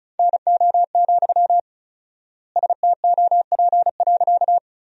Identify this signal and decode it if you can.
Morse code, heard as a keyed tone, about 36 wpm, 700 Hz. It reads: NO, STOP.
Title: 4x Word-Spacing - Sentences from Top 300 Words - 35wpm